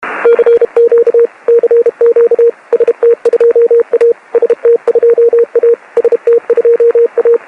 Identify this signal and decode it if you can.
Morse code, heard as a keyed tone, about 32 wpm, 470 Hz, with radio noise. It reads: CQ CQ ST2A ST2A ST2A
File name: ST2A CW